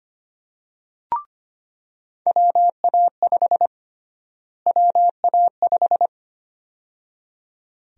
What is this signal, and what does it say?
Morse code, heard as a keyed tone, about 25 wpm, 700 Hz. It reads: WA5 WA5
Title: USA Call Sign Prefixes - Repeated 2x - 25wpm